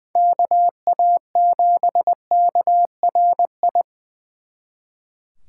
Morse code, heard as a keyed tone, about 20 wpm, 700 Hz. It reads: KA7KRI
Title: Speed Racing - USA Call Signs 20wpm